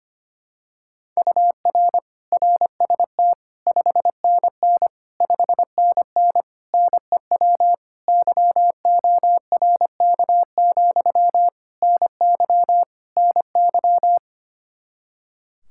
Morse code, heard as a keyed tone, about 25 wpm, 700 Hz. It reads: UR RST 5NN 5NN NEW YORK, NY NY
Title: Practice QSO - Exchange 1 - Beginning - 25wpm